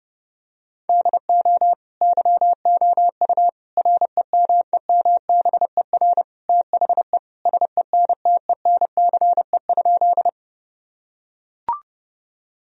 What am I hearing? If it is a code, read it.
Morse code, heard as a keyed tone, about 32 wpm, 700 Hz. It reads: DO YOU REMEMBER THE SENTENCE?
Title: Sentences from Top 1000 Words - Review Part 1 - 30wpm